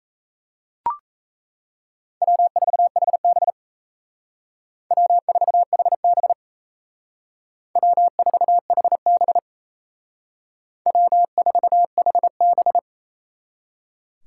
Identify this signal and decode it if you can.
Morse code, heard as a keyed tone, about 36 wpm, 700 Hz. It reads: W4HB W4HB W4HB W4HB